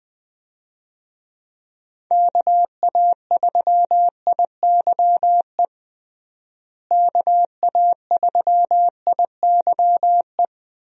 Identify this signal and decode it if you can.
Morse code, heard as a keyed tone, about 20 wpm, 700 Hz. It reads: KA3IYE KA3IYE
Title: USA Call Signs - 6 Characters - Repeated 2x - 20wpm